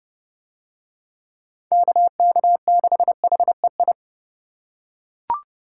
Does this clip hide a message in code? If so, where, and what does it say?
Yes: Morse code from 1.7 to 3.9 s, heard as a keyed tone, about 28 wpm, 700 Hz. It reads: KK6HEI